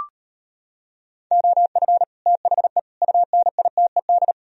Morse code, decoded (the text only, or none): OF THE UNITED